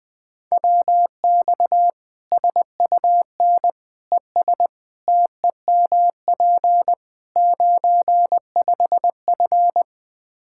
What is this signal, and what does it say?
Morse code, heard as a keyed tone, about 20 wpm, 700 Hz. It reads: WX SUN ES TEMP 95F
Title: Practice QSO - Exchange 2 - Info - 20wpm